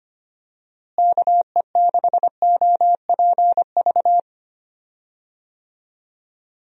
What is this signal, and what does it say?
Morse code, heard as a keyed tone, about 25 wpm, 700 Hz. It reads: KE6OPV
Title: USA Call Signs - 6 Characters - Repeated 2x - 25wpm